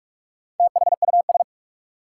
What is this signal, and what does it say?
Morse code, heard as a keyed tone, about 46 wpm, 700 Hz. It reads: THUS
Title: Sentences from Top 800 Words - New Word Drill - 45wpm